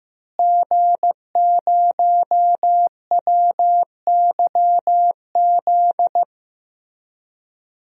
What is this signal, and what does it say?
Morse code, heard as a keyed tone, about 15 wpm, 700 Hz. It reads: G0WYZ
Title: DXCC-100 Call Signs - 5 Characters - Repeated 3x - 15wpm